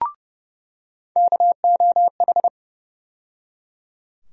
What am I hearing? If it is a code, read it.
Morse code, heard as a keyed tone, about 30 wpm, 700 Hz. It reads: KOH
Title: USA Call Sign Postfixes - 3 Letters 30wpm